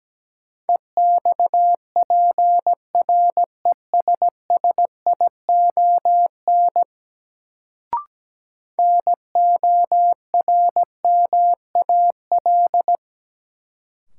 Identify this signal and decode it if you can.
Morse code, heard as a keyed tone, about 17 wpm, 700 Hz. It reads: EXPRESSION NORMAL
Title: Sentences from Top 1200 Words - New Word Drill Alt - 17wpm